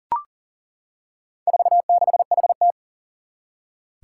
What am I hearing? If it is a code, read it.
Morse code, heard as a keyed tone, about 40 wpm, 700 Hz. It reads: 46HT
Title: Four Challenging Letter-Number Combinations 40wpm